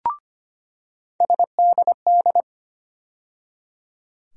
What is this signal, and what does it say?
Morse code, heard as a keyed tone, about 25 wpm, 700 Hz. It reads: SDD
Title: USA Call Sign Postfixes - 3 Letters 25wpm